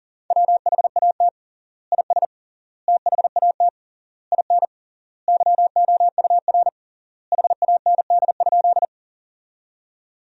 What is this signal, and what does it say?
Morse code, heard as a keyed tone, about 40 wpm, 700 Hz. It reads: WHAT IS THAT IN YOUR HAND?
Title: 3x Word-Spacing - Sentences from Top 200 Words - 40wpm